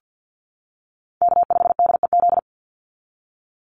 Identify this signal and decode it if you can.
Morse code, heard as a keyed tone, about 50 wpm, 700 Hz. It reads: K5DEZ
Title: NA SOTA Chaser Callsigns 50wpm